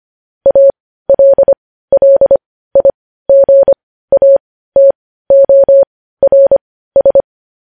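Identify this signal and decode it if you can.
Morse code, heard as a keyed tone, about 25 wpm, spaced as for 15 wpm, 550 Hz. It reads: ALLIGATORS
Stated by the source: Morse news headlines practice at 15 wpm.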